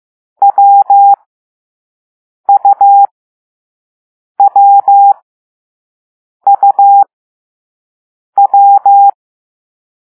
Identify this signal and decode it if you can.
Morse code, heard as a keyed tone, about 15 wpm, spaced as for 5 wpm, 800 Hz. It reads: WUWUW